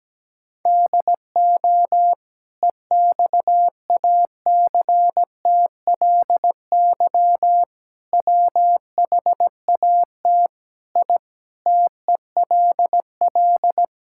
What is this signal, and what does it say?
Morse code, heard as a keyed tone, about 17 wpm, 700 Hz. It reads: DO EXACTLY WHAT I TELL
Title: Sentences from Top 1000 Words - Review Part 6 - 17wpm